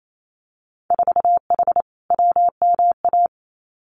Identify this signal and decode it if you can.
Morse code, heard as a keyed tone, about 28 wpm, 700 Hz. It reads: 4H WMA
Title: Winter Field Day Exchanges Encore 28wpm